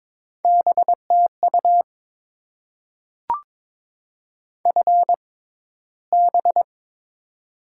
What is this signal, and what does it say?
Morse code, heard as a keyed tone, about 22 wpm, spaced as for 11 wpm, 690 Hz. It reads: B TU FB
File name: Top 20 QSO Elements - 6x Character Spacing - 22wpm